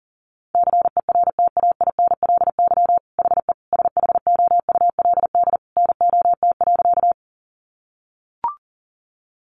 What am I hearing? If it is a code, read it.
Morse code, heard as a keyed tone, about 40 wpm, 700 Hz. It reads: CERTAINLY HE SHOULD NOT.
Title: Sentences from Top 900 Words 40wpm